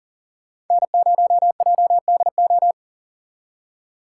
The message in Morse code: N0JDO

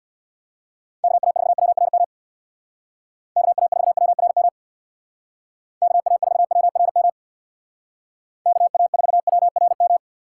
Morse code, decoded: KA4WRM KA4WRM KA4WRM KA4WRM

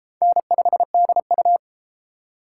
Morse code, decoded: N5DU

